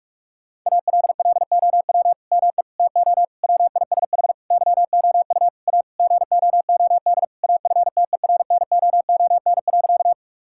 Morse code, heard as a keyed tone, about 47 wpm, 700 Hz. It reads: ALLOW ME TO WISH YOU A GOOD AFTERNOON.
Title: Sentences from Top 1600 Words 45wpm